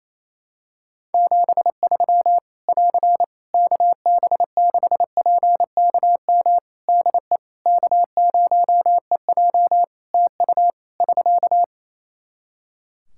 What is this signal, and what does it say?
Morse code, heard as a keyed tone, about 28 wpm, 700 Hz. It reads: 73 <AR> KB6PKM DE K0EJ TU <SK>
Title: Practice QSO - Exchange 4 - Ending - 28wpm